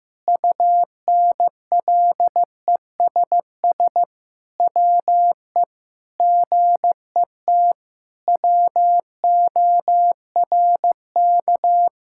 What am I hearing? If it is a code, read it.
Morse code, heard as a keyed tone, about 15 wpm, 700 Hz. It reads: UNLESS WE GET WORK
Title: Sentences from Top 1000 Words - Review Part 6 - 15wpm